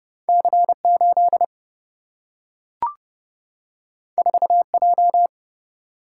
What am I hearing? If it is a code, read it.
Morse code, heard as a keyed tone, about 30 wpm, 700 Hz. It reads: C8 4J
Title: Single-Letter Single-Number 30wpm